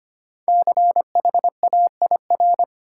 Morse code, heard as a keyed tone, about 25 wpm, 700 Hz. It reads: CHAIR
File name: 5 Letter Words - First 3 Letters Given - 25wpm